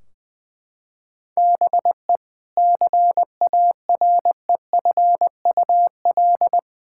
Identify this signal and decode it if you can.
Morse code, heard as a keyed tone, about 21 wpm, 700 Hz. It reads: BE CAREFUL
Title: Sentences from Top 1600 Words 20wpm